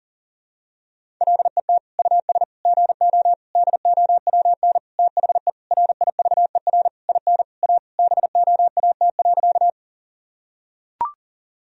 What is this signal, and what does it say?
Morse code, heard as a keyed tone, about 40 wpm, 700 Hz. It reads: LET US GO DOWN THE RIVER IN A BOAT.
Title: Sentences from Top 1000 Words - Review Part 3 - 40wpm